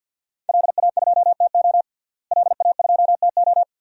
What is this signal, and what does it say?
Morse code, heard as a keyed tone, about 51 wpm, 700 Hz. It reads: LA2TO LA2TO
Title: DXCC-100 Call Signs - 5 Characters - Repeated 2x - 50wpm